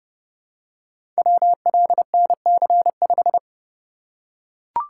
Morse code, heard as a keyed tone, about 31 wpm, 700 Hz. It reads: WLNC5
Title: Code Groups - 5 Letter:Numbers - 30wpm